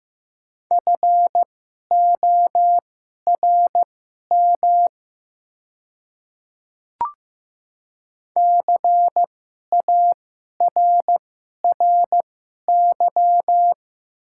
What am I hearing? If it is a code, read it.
Morse code, heard as a keyed tone, about 15 wpm, spaced as for 11 wpm, 700 Hz. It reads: FORM CARRY
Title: 2x Character Spacing - Top 500 Words - 15wpm